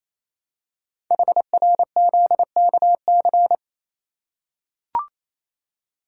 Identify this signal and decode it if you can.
Morse code, heard as a keyed tone, about 27 wpm, 700 Hz. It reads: HRZKC
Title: Code Groups - 5 Letters - 28wpm